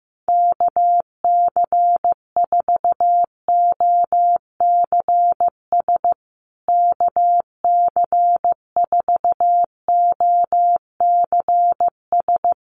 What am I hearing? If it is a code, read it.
Morse code, heard as a keyed tone, about 14 wpm, 700 Hz. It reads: KC4OCS KC4OCS
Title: USA Call Signs Repeated 2x 15wpm